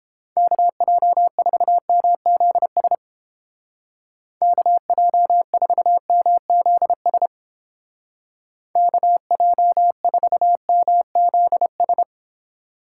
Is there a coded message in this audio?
Yes: Morse code from 0.4 to 12.0 s, heard as a keyed tone, about 31 wpm, 700 Hz. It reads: KJ4MZS KJ4MZS KJ4MZS